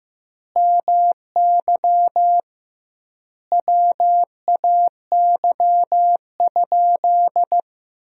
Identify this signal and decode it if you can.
Morse code, heard as a keyed tone, about 15 wpm, 700 Hz. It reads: MY WAY?
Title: 2x Word-Spacing - Sentences from Top 200 Words - 15wpm